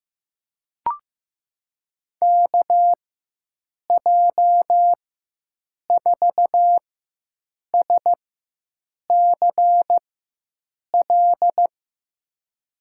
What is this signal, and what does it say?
Morse code, heard as a keyed tone, about 16 wpm, spaced as for 7 wpm, 700 Hz. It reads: KJ4SCL
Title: USA Call Signs - 6 Characters - 4x Character Spacing - 15wpm